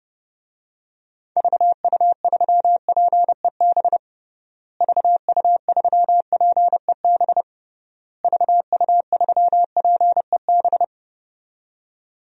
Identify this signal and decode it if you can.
Morse code, heard as a keyed tone, about 32 wpm, 700 Hz. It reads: VU3PEB VU3PEB VU3PEB
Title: DXCC-100 Call Signs - 6 Characters - Repeated 3x - 30wpm